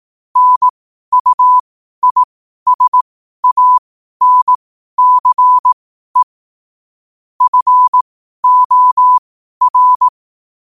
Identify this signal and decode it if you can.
Morse code, heard as a keyed tone, about 18 wpm, spaced as for 12 wpm, 1000 Hz. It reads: NUISANCE FOR
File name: qotdfiles / 12wpm